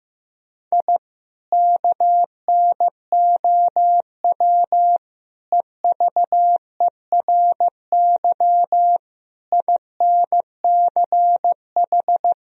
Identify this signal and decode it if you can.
Morse code, heard as a keyed tone, about 15 wpm, 700 Hz. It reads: I KNOW EVERY INCH
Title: Sentences from Top 1000 Words - Review Part 3 - 15wpm